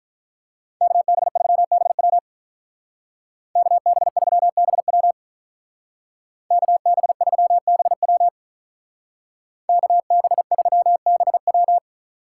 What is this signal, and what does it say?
Morse code, heard as a keyed tone, about 44 wpm, 700 Hz. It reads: KB3BW KB3BW KB3BW KB3BW